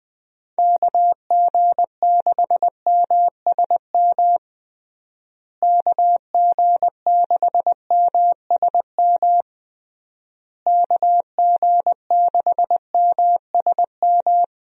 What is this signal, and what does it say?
Morse code, heard as a keyed tone, about 20 wpm, 700 Hz. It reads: KG6MSM KG6MSM KG6MSM
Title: USA Call Signs - 6 Characters - Repeated 3x - 20wpm